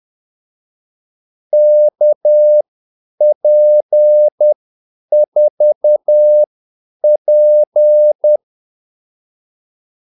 Morse code, heard as a keyed tone, about 10 wpm, spaced as for 8 wpm, 600 Hz. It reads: KP4P
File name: Callsigns-10-8wpm